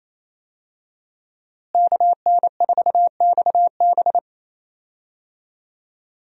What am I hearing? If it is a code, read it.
Morse code, heard as a keyed tone, about 27 wpm, 700 Hz. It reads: KN4XB
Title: USA Call Signs - 5 Characters - Repeated 3x - 28wpm